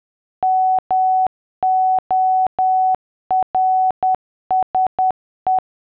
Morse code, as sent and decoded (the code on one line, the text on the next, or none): -- --- .-. ... .
MORSE